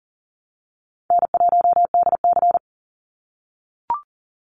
Morse code, heard as a keyed tone, about 39 wpm, 700 Hz. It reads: N1DC
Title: Call Signs - 40wpm